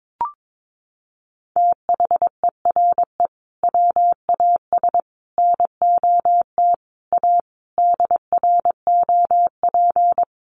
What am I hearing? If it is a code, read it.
Morse code, heard as a keyed tone, about 22 wpm, 700 Hz. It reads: THERE WAS NOT A DROP
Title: Sentences from Top 1400 Words 22wpm